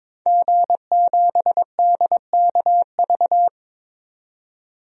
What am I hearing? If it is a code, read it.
Morse code, heard as a keyed tone, about 22 wpm, 700 Hz. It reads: G7DKV
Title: DXCC-100 Call Signs - 5 Characters - 22wpm